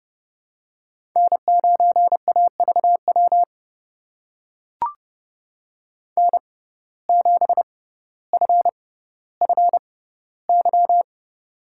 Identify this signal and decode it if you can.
Morse code, heard as a keyed tone, about 30 wpm, spaced as for 15 wpm, 700 Hz. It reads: N9AVW N7FFY
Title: USA Call Signs - 5 Characters - 6x Character Spacing - 30wpm